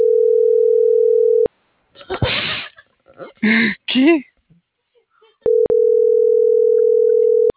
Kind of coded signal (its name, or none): telephone call progress tone